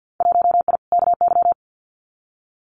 Morse code, heard as a keyed tone, about 50 wpm, 700 Hz. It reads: 1I KY
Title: Winter Field Day Exchanges 50wpm